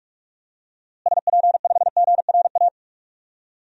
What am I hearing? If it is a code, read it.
Morse code, heard as a keyed tone, about 45 wpm, 700 Hz. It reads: SP5GRA
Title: DXCC-100 Call Signs - 6 Characters - Repeated 2x - 45wpm